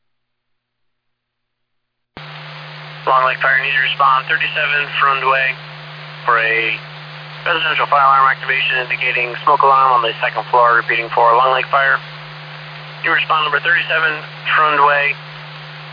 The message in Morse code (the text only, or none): none